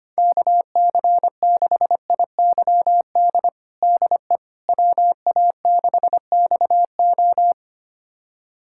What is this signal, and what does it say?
Morse code, heard as a keyed tone, about 25 wpm, 700 Hz. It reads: KC6IYD DE WA6XO
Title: USA Callsign DE Callsign 25wpm